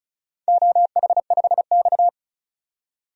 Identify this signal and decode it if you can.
Morse code, heard as a keyed tone, about 36 wpm, 700 Hz. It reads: OH5X
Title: DXCC-100 Call Signs - 4 Characters - 35wpm